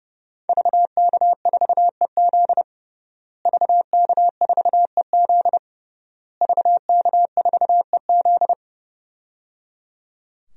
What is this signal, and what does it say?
Morse code, heard as a keyed tone, about 30 wpm, 700 Hz. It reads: VK4EZ VK4EZ VK4EZ
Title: DXCC-100 Call Signs - 5 Characters - Repeated 3x - 30wpm